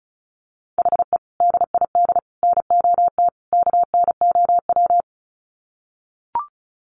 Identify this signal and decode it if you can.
Morse code, heard as a keyed tone, about 34 wpm, 700 Hz. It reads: HE DID NOT KNOW
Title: Sets of 4 Words 35wpm